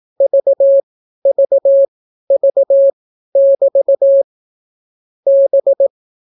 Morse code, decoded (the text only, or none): VVV<BT> B